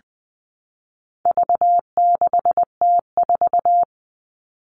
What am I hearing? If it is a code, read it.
Morse code, heard as a keyed tone, about 20 wpm, 700 Hz. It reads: V6T4
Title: All Cut Numbers - 4-Digit 20wpm